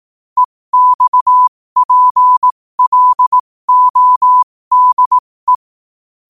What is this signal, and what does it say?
Morse code, heard as a keyed tone, about 18 wpm, spaced as for 15 wpm, 1000 Hz. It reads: EXPLODE
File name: qotdfiles / 15wpm